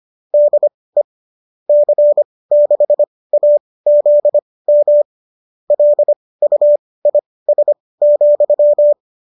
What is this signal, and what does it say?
Morse code, heard as a keyed tone, about 25 wpm, spaced as for 18 wpm, 570 Hz. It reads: DE C6AZM LUIS,